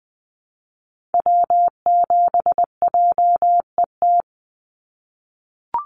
Morse code, heard as a keyed tone, about 20 wpm, 700 Hz. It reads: W7JET
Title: NA SOTA Activator Callsigns 20wpm